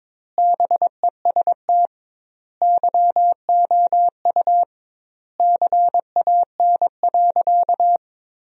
Morse code, heard as a keyed tone, about 22 wpm, 700 Hz. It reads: BEST YOU CAN.
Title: 2x Word-Spacing - Sentences from Top 500 Words - 22wpm